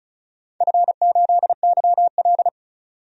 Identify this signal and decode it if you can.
Morse code, heard as a keyed tone, about 35 wpm, 690 Hz. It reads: F8YL
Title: DXCC-100 Call Signs - 4 Characters - 35wpm